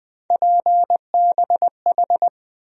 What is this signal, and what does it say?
Morse code, heard as a keyed tone, about 20 wpm, 700 Hz. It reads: PBH
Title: USA Call Sign Postfixes - 3 Letters 20wpm